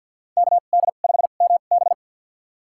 Morse code, heard as a keyed tone, about 50 wpm, spaced as for 36 wpm, 700 Hz. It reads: KD5MB
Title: USA Call Signs - 5 Characters - 2x Character Spacing - 50wpm